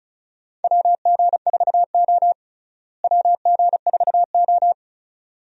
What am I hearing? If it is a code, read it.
Morse code, heard as a keyed tone, about 36 wpm, 700 Hz. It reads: WG4O WG4O
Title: USA Call Signs - 4 Characters - Repeated 2x - 35wpm